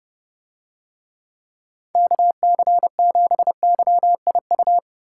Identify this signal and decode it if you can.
Morse code, heard as a keyed tone, about 29 wpm, 700 Hz. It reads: KC7YIU